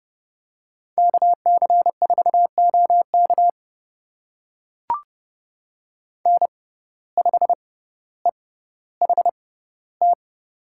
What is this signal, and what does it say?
Morse code, heard as a keyed tone, about 31 wpm, spaced as for 16 wpm, 700 Hz. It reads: KC4OK N5EHT
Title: USA Call Signs - 6x Character Spacing - 30wpm